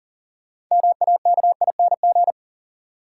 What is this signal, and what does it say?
Morse code, heard as a keyed tone, about 40 wpm, 700 Hz. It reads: MAKING
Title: Sentences from Top 1000 Words - Review Part 6 - New Word Drill - 40wpm